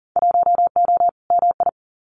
Morse code, heard as a keyed tone, about 41 wpm, 700 Hz. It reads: 1O MI
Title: Winter Field Day Exchanges Encore 40wpm